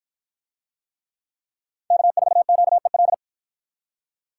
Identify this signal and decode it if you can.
Morse code, heard as a keyed tone, about 53 wpm, 700 Hz. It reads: K4QEL